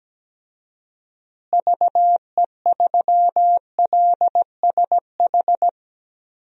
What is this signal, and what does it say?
Morse code, heard as a keyed tone, about 17 wpm, 700 Hz. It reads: VE3LSH
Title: DXCC-100 Call Signs - 6 Characters - Repeated 3x - 17wpm